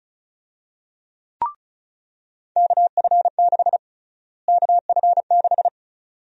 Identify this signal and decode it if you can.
Morse code, heard as a keyed tone, about 36 wpm, 700 Hz. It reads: KF6 KF6
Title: USA Call Sign Prefixes - Repeated 2x - 35wpm